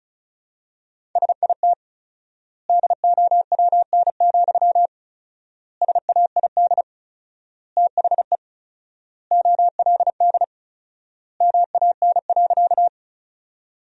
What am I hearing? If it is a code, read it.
Morse code, heard as a keyed tone, about 36 wpm, 700 Hz. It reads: SIT DOWN, SAID THE OLD MAN.
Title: 4x Word-Spacing - Sentences from Top 500 Words - 35wpm